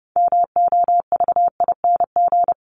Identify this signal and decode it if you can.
Morse code, heard as a keyed tone, about 30 wpm, 700 Hz. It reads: MOVING
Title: ING Suffix - Top 100 Words with Base Words 30wpm